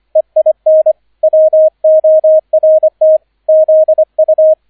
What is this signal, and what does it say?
Morse code, heard as a keyed tone, about 25 wpm, 630 Hz, with radio noise. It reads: EIN WORT ZU
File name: 003 Dritter Streich 25wpm